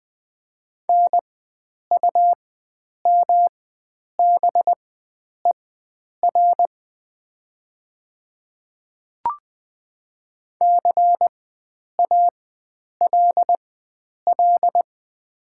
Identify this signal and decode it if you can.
Morse code, heard as a keyed tone, about 20 wpm, spaced as for 9 wpm, 700 Hz. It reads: NUMBER CALL
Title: 4x Character Spacing - Top 100 Words - 20wpm